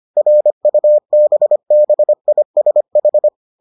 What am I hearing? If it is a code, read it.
Morse code, heard as a keyed tone, about 25 wpm, 600 Hz. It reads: RUBBISH